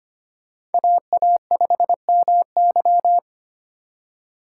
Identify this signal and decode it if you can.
Morse code, heard as a keyed tone, about 31 wpm, 700 Hz. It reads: AA5MY
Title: Speed Racing - USA 2x2 Call Signs 25wpm